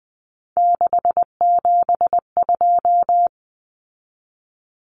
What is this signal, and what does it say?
Morse code, heard as a keyed tone, about 19 wpm, 700 Hz. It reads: 672
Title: Numbers - 3-Digit 20wpm